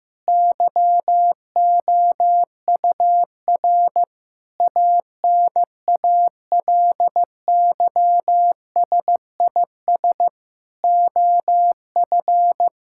Morse code, heard as a keyed tone, about 14 wpm, 700 Hz. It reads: YOUR ANALYSIS OF
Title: Sentences from Top 1000 Words - Review Part 5 - 15wpm